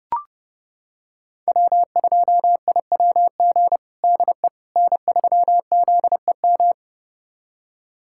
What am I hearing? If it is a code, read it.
Morse code, heard as a keyed tone, about 30 wpm, 700 Hz. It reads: W2IWG DE N3ZEM
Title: USA Callsign DE Callsign - 5 Characters - 30wpm